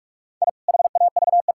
Morse code, heard as a keyed tone, about 41 wpm, 700 Hz. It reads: I HAVE
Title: Pangram Challenge v2 - 45wpm